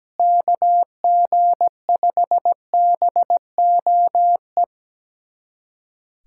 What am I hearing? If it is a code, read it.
Morse code, heard as a keyed tone, about 17 wpm, 700 Hz. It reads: KG5BOE
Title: USA Callsign DE Callsign - 6 Characters - 17wpm